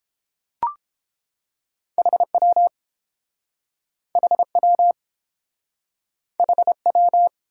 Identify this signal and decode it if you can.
Morse code, heard as a keyed tone, about 30 wpm, 700 Hz. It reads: HW HW HW